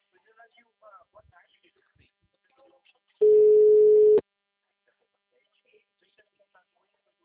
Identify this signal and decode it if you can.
Ringback tone (Europe, 425 Hz), with speech.
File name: out- -2012- - - .12